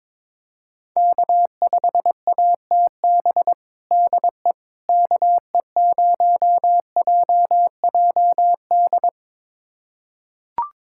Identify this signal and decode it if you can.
Morse code, heard as a keyed tone, about 22 wpm, 700 Hz. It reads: K5ATB DE KE0JJD
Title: USA Callsign DE Callsign 22wpm